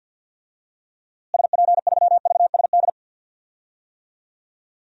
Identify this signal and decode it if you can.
Morse code, heard as a keyed tone, about 52 wpm, 700 Hz. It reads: SP3VSD
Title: DXCC-100 Call Signs - 6 Characters - Repeated 3x - 50wpm